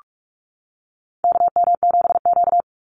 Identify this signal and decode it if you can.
Morse code, heard as a keyed tone, about 44 wpm, 700 Hz. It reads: KM7Q
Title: Call Signs - 45wpm